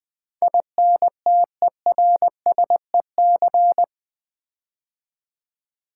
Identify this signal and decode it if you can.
Morse code, heard as a keyed tone, about 20 wpm, 700 Hz. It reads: INTERSEC
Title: Build A Word - 4 Letters At A Time - 12-Letter Words - 20wpm